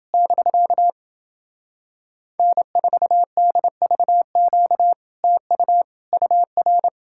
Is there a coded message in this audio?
Yes: Morse code from 0.1 to 6.9 s, heard as a keyed tone, about 30 wpm, 700 Hz. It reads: <BK> N4DVQ TU UR